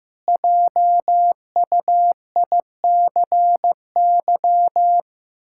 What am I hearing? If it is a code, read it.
Morse code, heard as a keyed tone, about 15 wpm, 700 Hz. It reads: JUICY
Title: Pangram Challenge v2 - 15wpm